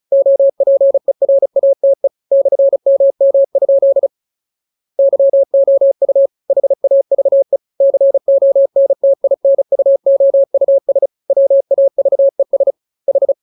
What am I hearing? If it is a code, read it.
Morse code, heard as a keyed tone, about 34 wpm, 550 Hz. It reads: OPERATE /MM? YOU HAVE CONTINUOUS WAVES H
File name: jokes / joke 9 35 0000